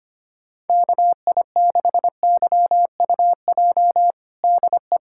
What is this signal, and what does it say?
Morse code, heard as a keyed tone, about 25 wpm, 700 Hz. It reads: KI6YUJ DE
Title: USA Callsign DE Callsign 25wpm